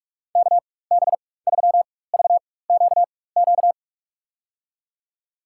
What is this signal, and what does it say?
Morse code, heard as a keyed tone, about 46 wpm, spaced as for 21 wpm, 700 Hz. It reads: KB3VQQ
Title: USA Call Signs - 6 Characters - 4x Character Spacing - 45wpm